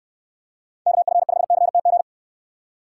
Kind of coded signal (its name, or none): Morse code